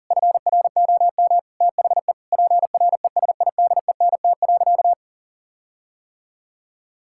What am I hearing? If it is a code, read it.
Morse code, heard as a keyed tone, about 37 wpm, 700 Hz. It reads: FROM THE PRESIDENT.
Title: Sentences from Top 1600 Words 40wpm